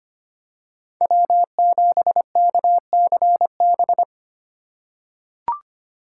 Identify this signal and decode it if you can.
Morse code, heard as a keyed tone, about 25 wpm, 700 Hz. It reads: W7KCB
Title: USA Call Signs - 5 Characters - 2x Character Spacing - 25wpm